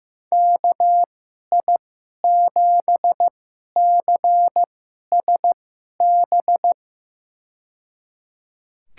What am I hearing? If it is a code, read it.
Morse code, heard as a keyed tone, about 15 wpm, spaced as for 11 wpm, 700 Hz. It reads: KI7CSB
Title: USA Call Signs - 6 Characters - 2x Character Spacing - 15wpm